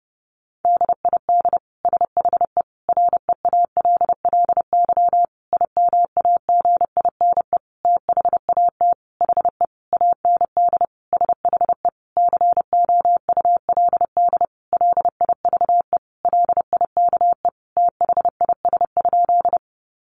Morse code, heard as a keyed tone, about 30 wpm, 700 Hz. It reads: DID SHE REALLY IMAGINE THAT HE AND SHE COULD LIVE LIKE THIS?
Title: Sentences from Top 1400 Words 30wpm